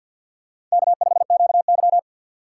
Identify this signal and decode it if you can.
Morse code, heard as a keyed tone, about 53 wpm, 700 Hz. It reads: K5QY
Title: USA Call Signs - 4x Character Spacing - 50wpm